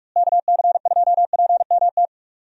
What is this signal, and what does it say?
Morse code, heard as a keyed tone, about 44 wpm, 700 Hz. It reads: KC2PMT
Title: USA Call Signs - 6 Characters - Repeated 2x - 45wpm